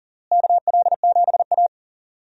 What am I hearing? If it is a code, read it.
Morse code, heard as a keyed tone, about 40 wpm, 700 Hz. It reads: KL7A
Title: DXCC-100 Call Signs - 4 Characters - Repeated 3x - 40wpm